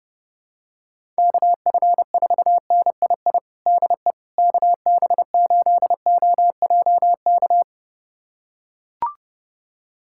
Morse code, heard as a keyed tone, about 32 wpm, 700 Hz. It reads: KF4NII DE KB8OJK
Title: USA Callsign DE Callsign - 6 Characters - 30wpm